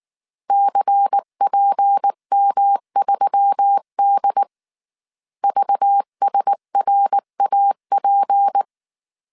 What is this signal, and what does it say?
Morse code, heard as a keyed tone, about 19 wpm, 800 Hz. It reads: CPM3D VSRAP